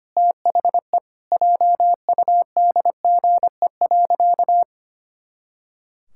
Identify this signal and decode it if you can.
Morse code, heard as a keyed tone, about 25 wpm, 700 Hz. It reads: THE JUDGE.
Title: Pangram Challenge v2 - 25wpm